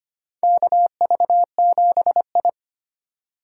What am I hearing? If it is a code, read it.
Morse code, heard as a keyed tone, about 25 wpm, 700 Hz. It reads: KV7I
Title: USA Call Signs - 4 Characters - 4x Character Spacing - 25wpm